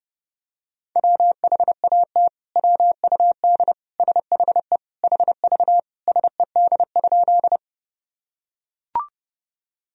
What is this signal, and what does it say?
Morse code, heard as a keyed tone, about 32 wpm, 700 Hz. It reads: WHAT WUD SHE HV SED?
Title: Sentences from Top 100 Words and Abbreviations 30wpm